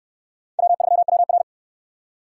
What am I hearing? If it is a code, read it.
Morse code, heard as a keyed tone, about 68 wpm, 700 Hz. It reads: W3WR